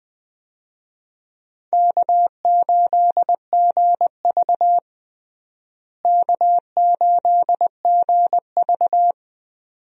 Morse code, heard as a keyed tone, about 21 wpm, 700 Hz. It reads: K8GV K8GV
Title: USA Call Signs - 4 Characters - Repeated 2x - 20wpm